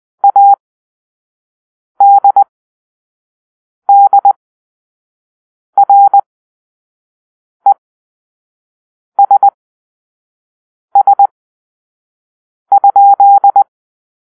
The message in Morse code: ADDRESS?